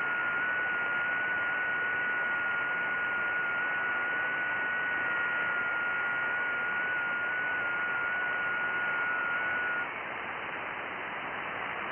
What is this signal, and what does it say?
FT8.
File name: IC-7100